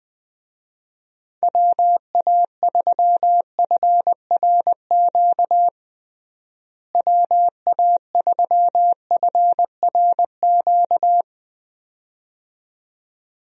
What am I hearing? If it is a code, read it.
Morse code, heard as a keyed tone, about 20 wpm, 700 Hz. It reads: WA3FRQ WA3FRQ
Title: USA Call Signs - 6 Characters - Repeated 2x - 20wpm